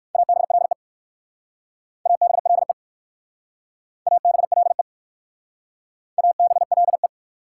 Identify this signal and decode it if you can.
Morse code, heard as a keyed tone, about 57 wpm, 700 Hz. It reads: ABLE ABLE ABLE ABLE